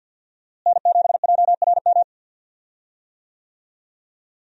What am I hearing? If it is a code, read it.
Morse code, heard as a keyed tone, about 46 wpm, 700 Hz. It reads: N7JRM
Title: USA Call Signs - 5 Characters - Repeated 2x - 50wpm